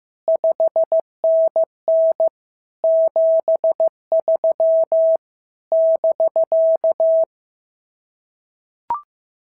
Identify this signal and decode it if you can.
Morse code, heard as a keyed tone, about 15 wpm, 650 Hz. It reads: 5NN 73 <BK>
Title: SOTA QSO Practice 15wpm